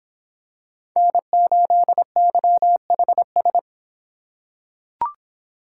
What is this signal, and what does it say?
Morse code, heard as a keyed tone, about 26 wpm, 700 Hz. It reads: N8YHS